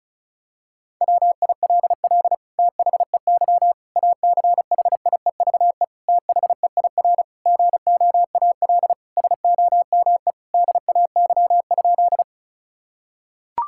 Morse code, heard as a keyed tone, about 36 wpm, 700 Hz. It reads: WILL THEY ACHIEVE THEIR GOAL SOME DAY?
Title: Sentences from Top 1000 Words - Review Part 6 - 35wpm